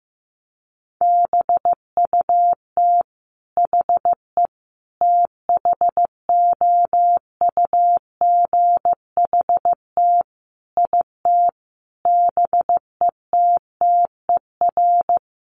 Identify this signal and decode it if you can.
Morse code, heard as a keyed tone, about 15 wpm, 700 Hz. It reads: BUT HE THOUGHT IT BETTER
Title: Sentences from Top 1400 Words 15wpm